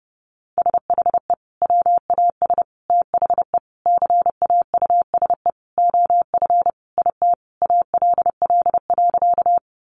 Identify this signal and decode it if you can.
Morse code, heard as a keyed tone, about 30 wpm, 700 Hz. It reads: SHE WAS THE CAUSE OF IT ALL.
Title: Sentences from Top 200 Words 30wpm